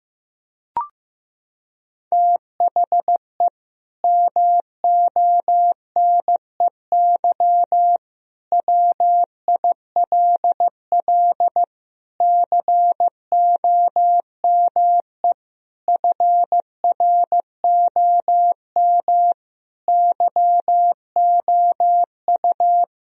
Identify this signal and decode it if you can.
Morse code, heard as a keyed tone, about 15 wpm, 700 Hz. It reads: THE MONEY WILL COME FROM YOU
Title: Sentences from Top 1000 Words - Review Part 4 - 15wpm